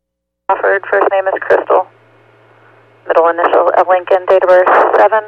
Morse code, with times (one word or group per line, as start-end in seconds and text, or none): none